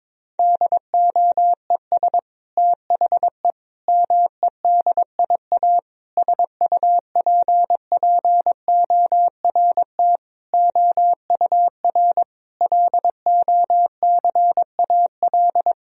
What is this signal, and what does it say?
Morse code, heard as a keyed tone, about 22 wpm, 700 Hz. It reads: DOES THE MEDIA SUPPORT OUR LOCAL
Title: Sentences from Top 1000 Words - Review Part 4 - 22wpm